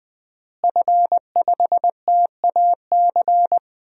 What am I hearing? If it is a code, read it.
Morse code, heard as a keyed tone, about 20 wpm, 700 Hz. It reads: F5TAC
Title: DXCC-100 Call Signs - 5 Characters - Repeated 2x - 20wpm